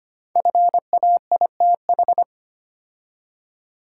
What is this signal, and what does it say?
Morse code, heard as a keyed tone, about 25 wpm, 700 Hz. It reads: FAITH
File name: 5 Letter Words - First 3 Letters Given - 25wpm